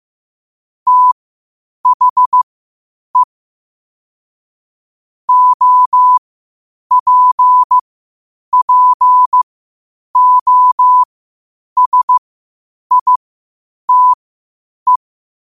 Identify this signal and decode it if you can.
Morse code, heard as a keyed tone, about 15 wpm, spaced as for 8 wpm, 1000 Hz. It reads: THE OPPOSITE
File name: qotdfiles / 8wpm